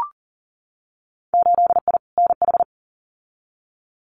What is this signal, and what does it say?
Morse code, heard as a keyed tone, about 40 wpm, 700 Hz. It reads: 8I NH
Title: Winter Field Day Exchanges Encore 40wpm